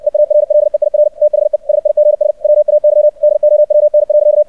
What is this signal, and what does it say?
Morse code, heard as a keyed tone, about 64 wpm, 600 Hz, with radio noise. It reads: EARLEIR THE SIGS WAY DOWN.
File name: archived / CRSnet